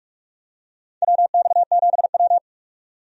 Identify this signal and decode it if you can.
Morse code, heard as a keyed tone, about 46 wpm, 700 Hz. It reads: WX7W
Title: USA Call Signs - 4 Characters - 45wpm